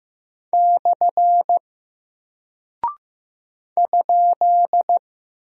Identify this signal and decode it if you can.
Morse code, heard as a keyed tone, about 15 wpm, 700 Hz. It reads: / ?
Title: Single Character (Letter:Number:Punctuation:Prosign) Alt 15wpm